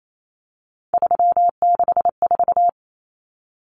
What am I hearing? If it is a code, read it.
Morse code, heard as a keyed tone, about 28 wpm, 700 Hz. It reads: 364
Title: Common Cut Numbers - 3-Digit 28wpm